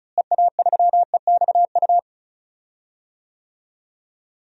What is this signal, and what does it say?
Morse code, heard as a keyed tone, about 36 wpm, 700 Hz. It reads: EA3EXU
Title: DXCC-100 Call Signs - 6 Characters - Repeated 2x - 35wpm